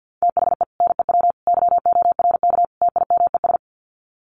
Morse code, heard as a keyed tone, about 50 wpm, 700 Hz. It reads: THE NEW YORK TIMES
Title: Sets of 4 Words 50wpm